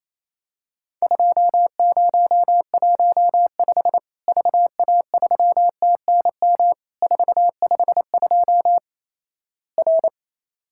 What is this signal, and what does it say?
Morse code, heard as a keyed tone, about 28 wpm, 700 Hz. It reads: 2015 VA3TNM 452 R
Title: Worked All Europe DX Contest - QTC Exchange - 28wpm